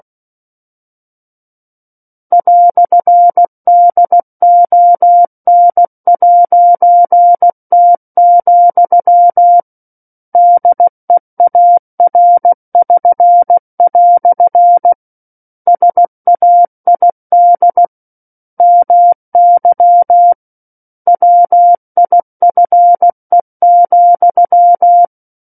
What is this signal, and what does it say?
Morse code, heard as a keyed tone, about 16 wpm, 700 Hz. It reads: "DON'T, DEAR<SN>" SAID MY WIFE,
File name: Book One / 07-Chapter 7 - 16 WPM